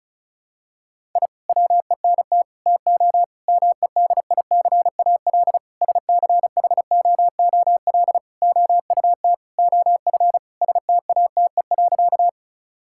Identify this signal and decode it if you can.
Morse code, heard as a keyed tone, about 36 wpm, 700 Hz. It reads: I WENT TO MEDICAL SCHOOL OUT OF STATE.
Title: Sentences from Top 1000 Words - Review Part 5 - 35wpm